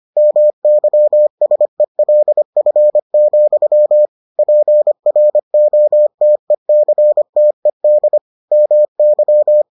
Morse code, heard as a keyed tone, about 25 wpm, 600 Hz. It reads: MYSELF, PROTECTED MY